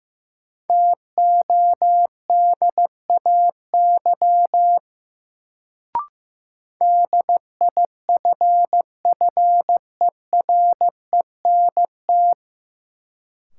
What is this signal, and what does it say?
Morse code, heard as a keyed tone, about 15 wpm, 700 Hz. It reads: TODAY DIFFERENT
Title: Sentences from Top 600 Words - New Word Drill - 15wpm